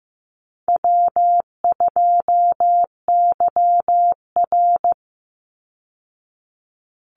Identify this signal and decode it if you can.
Morse code, heard as a keyed tone, about 15 wpm, 700 Hz. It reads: W2YR
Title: Call Signs Repeated 2x - 15wpm